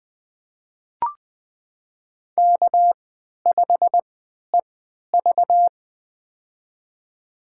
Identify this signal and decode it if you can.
Morse code, heard as a keyed tone, about 20 wpm, spaced as for 11 wpm, 700 Hz. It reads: K5EV
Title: USA Call Signs - 4 Characters - 3x Character Spacing - 20wpm